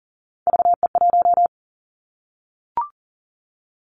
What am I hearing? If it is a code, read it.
Morse code, heard as a keyed tone, about 38 wpm, 700 Hz. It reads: VE1
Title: Callsign Prefixes for Top 50 DXCC - 40wpm